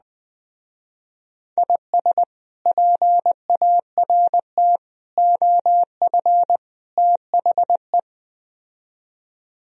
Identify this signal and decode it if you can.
Morse code, heard as a keyed tone, about 20 wpm, 700 Hz. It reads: IS PART OF THE
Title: Speed-Racing - Sets of 4 Words 17wpm